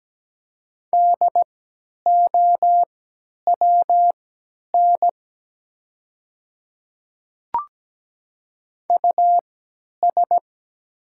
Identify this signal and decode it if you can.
Morse code, heard as a keyed tone, about 17 wpm, spaced as for 10 wpm, 700 Hz. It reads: DOWN US
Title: 3x Character Spacing - Top 300 Words - 17wpm